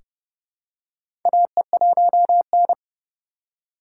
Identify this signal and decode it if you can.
Morse code, heard as a keyed tone, about 29 wpm, 700 Hz. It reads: AE1N
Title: K1USN SST QSO Practice - 30wpm